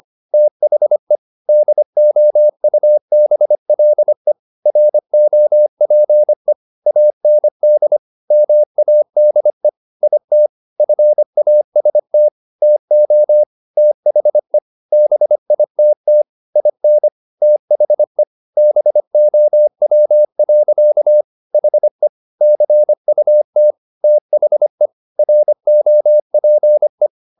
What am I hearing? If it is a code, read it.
Morse code, heard as a keyed tone, about 25 wpm, 600 Hz. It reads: THE DOUBLE ROPE AND MADE IT FAST TO THE BITT IN THE BOW. HE CUT THE ROPE